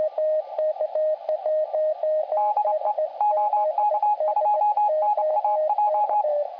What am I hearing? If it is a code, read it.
Morse code, heard as a keyed tone, about 27 wpm, 840 Hz, with radio noise. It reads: DE OK2NAJ